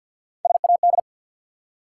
Morse code, heard as a keyed tone, about 49 wpm, 700 Hz. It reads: SAD
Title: Three Letter Words - 50wpm